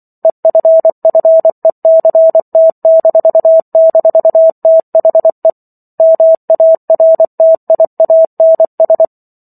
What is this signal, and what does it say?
Morse code, heard as a keyed tone, about 24 wpm, 650 Hz. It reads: EFFECT--THE MARTIANS